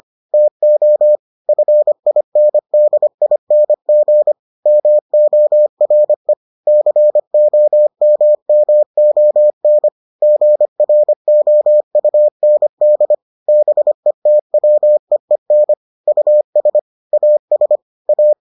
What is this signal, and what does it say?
Morse code, heard as a keyed tone, about 25 wpm, 600 Hz. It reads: TO FINDING MORE COMMON GROUND BETWEEN US AS A